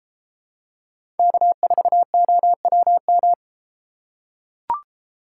Morse code, heard as a keyed tone, about 33 wpm, 700 Hz. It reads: K4OWM